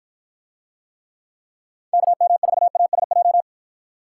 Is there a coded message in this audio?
Yes: Morse code from 1.9 to 3.4 s, heard as a keyed tone, about 49 wpm, 700 Hz. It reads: KM4ASJ